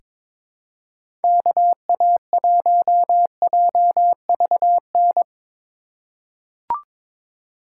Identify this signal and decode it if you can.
Morse code, heard as a keyed tone, about 22 wpm, 700 Hz. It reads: KA1JVN
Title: USA Call Signs - 6 Characters - 2x Character Spacing - 22wpm